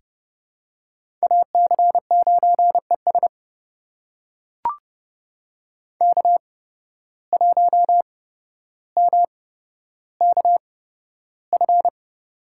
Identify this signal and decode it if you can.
Morse code, heard as a keyed tone, about 30 wpm, spaced as for 13 wpm, 700 Hz. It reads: AC9ES K1MKF
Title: USA Call Signs - 5 Characters - 8x Character Spacing - 30wpm